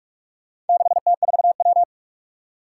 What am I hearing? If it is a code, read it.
Morse code, heard as a keyed tone, about 41 wpm, 700 Hz. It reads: 6T4W
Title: Four Challenging Letter-Number Combinations 45wpm